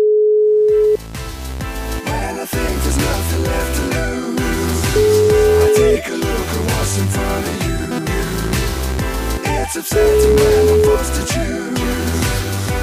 Ringback tone (Europe, 425 Hz), with noise.